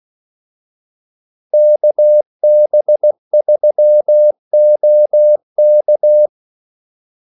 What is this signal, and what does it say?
Morse code, heard as a keyed tone, about 16 wpm, 600 Hz. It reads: KB3OK
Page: Kaldesignaler 16 wpm